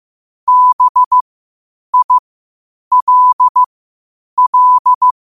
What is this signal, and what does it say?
Morse code, heard as a keyed tone, about 15 wpm, spaced as for 8 wpm, 1000 Hz. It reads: BILL
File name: qotdfiles / 8wpm